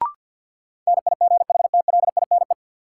Morse code, heard as a keyed tone, about 50 wpm, 700 Hz. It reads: NIGHTLINE
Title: Sets of 5 Words - New Word Drill - 50wpm